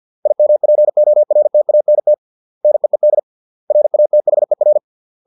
Morse code, heard as a keyed tone, about 50 wpm, 600 Hz. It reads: IMPORTANT NEED RATHER